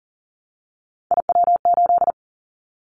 Morse code, heard as a keyed tone, about 40 wpm, 700 Hz. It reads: IW8
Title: Callsign Prefixes for Top 25 DXCC - 40wpm